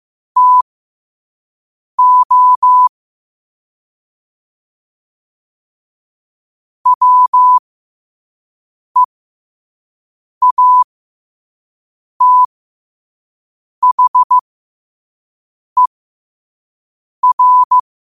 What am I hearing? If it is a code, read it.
Morse code, heard as a keyed tone, about 15 wpm, spaced as for 5 wpm, 1000 Hz. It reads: TO WEATHER
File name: qotdfiles / 5wpm